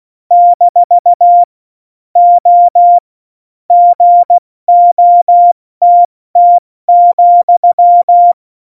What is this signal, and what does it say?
Morse code, heard as a keyed tone, about 16 wpm, 700 Hz. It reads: - O GOTT,
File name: Morsekurs / Witze / Witz 10. Oktober 17, Tempo 70